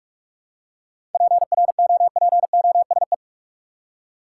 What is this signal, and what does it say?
Morse code, heard as a keyed tone, about 42 wpm, 700 Hz. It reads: PROPOSE
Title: Seven Letter Words - 45wpm